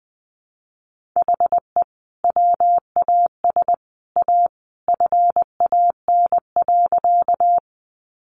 Morse code, heard as a keyed tone, about 20 wpm, 700 Hz. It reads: HE WAS A FAN.
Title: Sentences from Top 1300 Words 20wpm